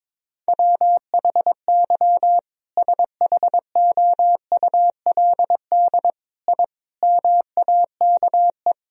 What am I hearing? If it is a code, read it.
Morse code, heard as a keyed tone, about 22 wpm, 700 Hz. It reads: WHY SHOULD I MAKE
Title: Sentences from Top 1000 Words - Review Part 3 - 22wpm